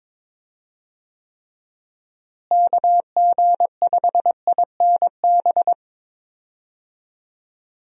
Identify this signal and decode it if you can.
Morse code, heard as a keyed tone, about 22 wpm, 700 Hz. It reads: KG5INB
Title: USA Call Signs - 6 Characters - Repeated 2x - 22wpm